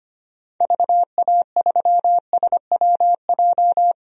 Morse code, heard as a keyed tone, about 25 wpm, 700 Hz. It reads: VA3SWJ
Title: DXCC-100 Call Signs - 6 Characters - 25wpm